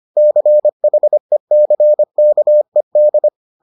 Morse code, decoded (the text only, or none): CHECKED